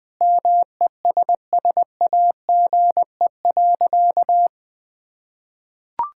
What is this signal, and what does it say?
Morse code, heard as a keyed tone, about 20 wpm, 700 Hz. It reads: MESSAGE.
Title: Sentences with 9-Word Phrases - 20wpm